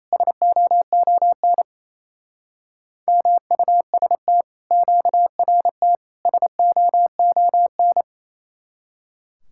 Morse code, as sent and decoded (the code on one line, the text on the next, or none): ... --- --- -. / -- ..- ... - / --.- .-. - / ... --- --- -.
SOON MUST QRT SOON